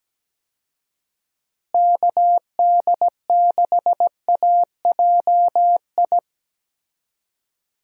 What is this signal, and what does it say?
Morse code, heard as a keyed tone, about 17 wpm, 700 Hz. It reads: KD6AJI
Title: USA Call Signs - 6 Characters - Repeated 2x - 17wpm